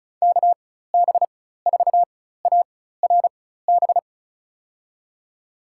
Morse code, heard as a keyed tone, about 36 wpm, spaced as for 16 wpm, 700 Hz. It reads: KB4ARB
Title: USA Call Signs - 6 Characters - 4x Character Spacing - 35wpm